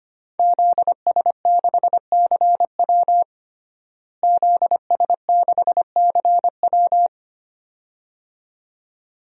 Morse code, heard as a keyed tone, about 25 wpm, 700 Hz. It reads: ZS6CW ZS6CW
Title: DXCC-100 Call Signs - 5 Characters - Repeated 2x - 25wpm